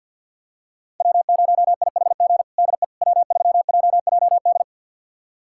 Morse code, heard as a keyed tone, about 52 wpm, 700 Hz. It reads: W0IHG DE W3JJD
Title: USA Callsign DE Callsign - 5 Characters - 50wpm